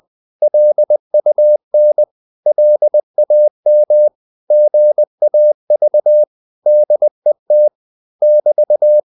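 Morse code, heard as a keyed tone, about 20 wpm, 600 Hz. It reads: LUN LAM GAV DET <BT>
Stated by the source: Morse code practice of short Danish words at 20 wpm, from CW med Gnister.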